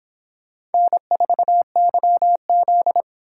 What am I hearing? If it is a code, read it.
Morse code, heard as a keyed tone, about 24 wpm, 700 Hz. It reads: N4YZ